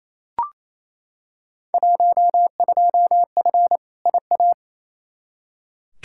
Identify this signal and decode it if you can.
Morse code, heard as a keyed tone, about 28 wpm, 700 Hz. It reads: 12F IA
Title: Field Day Exchanges Encore - 28wpm